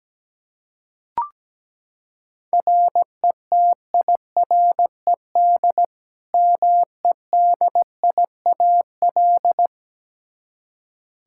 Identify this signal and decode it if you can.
Morse code, heard as a keyed tone, about 17 wpm, 700 Hz. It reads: RETIRED MEDIAL
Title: Practice QSO - Exchange 3 - More Info - 17wpm